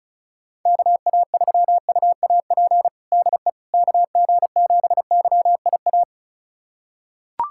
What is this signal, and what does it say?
Morse code, heard as a keyed tone, about 36 wpm, 700 Hz. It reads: KA3UAP DE KG7YIA
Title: USA Callsign DE Callsign - 6 Characters - 35wpm